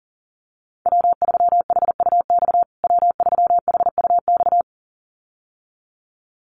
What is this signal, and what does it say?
Morse code, heard as a keyed tone, about 40 wpm, 700 Hz. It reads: W3HUX W3HUX
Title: USA Call Signs Repeated 2x 40wpm